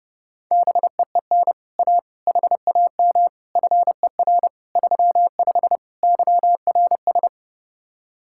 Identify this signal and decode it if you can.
Morse code, heard as a keyed tone, about 30 wpm, 700 Hz. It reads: BEEN A HAM FER 35 YRS